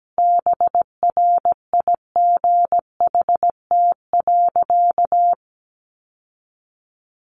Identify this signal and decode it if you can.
Morse code, heard as a keyed tone, about 17 wpm, 700 Hz. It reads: BRIGHT.
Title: Sentences from Top 1500 Words 17wpm